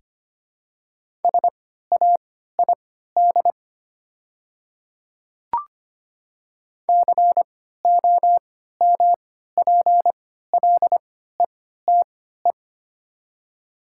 Morse code, heard as a keyed tone, about 25 wpm, spaced as for 14 wpm, 700 Hz. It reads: SAID COMPLETE
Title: 3x Character Spacing - Top 500 Words - 25wpm